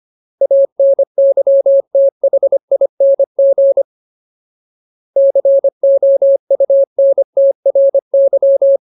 Morse code, heard as a keyed tone, about 25 wpm, 550 Hz. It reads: ANYTHING COUNTRY
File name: Ultimate Word List / 25WPM Ultimate